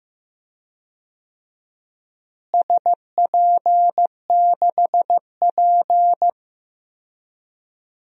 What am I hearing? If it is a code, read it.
Morse code, heard as a keyed tone, about 15 wpm, 700 Hz. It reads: SP6P
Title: DXCC-100 Call Signs - 4 Characters - Repeated 2x - 15wpm